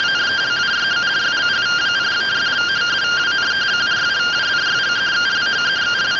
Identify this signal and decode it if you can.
Two-tone teleprinter signal, about 170 Hz shift.